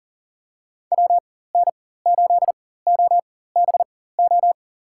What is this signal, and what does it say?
Morse code, heard as a keyed tone, about 40 wpm, spaced as for 19 wpm, 690 Hz. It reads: WN8OBO
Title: USA Call Signs - 4x Character Spacing - 40wpm